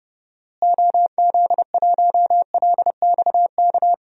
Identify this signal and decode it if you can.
Morse code, heard as a keyed tone, about 31 wpm, 700 Hz. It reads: OZ1LXK
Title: DXCC-100 Call Signs - 6 Characters - Repeated 2x - 30wpm